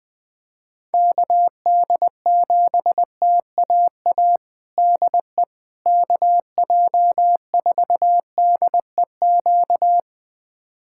Morse code, heard as a keyed tone, about 20 wpm, 700 Hz. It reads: KD7TAA DE KJ4DEQ
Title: USA Callsign DE Callsign 20wpm